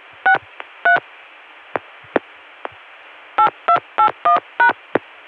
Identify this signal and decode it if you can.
DTMF, heard as telephone touch tones, tones of about 100 ms. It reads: 63 8281#